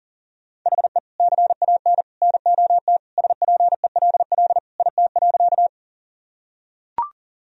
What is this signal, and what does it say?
Morse code, heard as a keyed tone, about 40 wpm, 700 Hz. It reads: HE CAN NOT SPELL IT.
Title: Sentences from Top 1000 Words - Review Part 1 - 40wpm